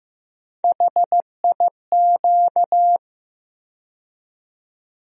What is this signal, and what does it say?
Morse code, heard as a keyed tone, about 15 wpm, 700 Hz. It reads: HIQ
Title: USA Call Sign Postfixes - 3 Letters 15wpm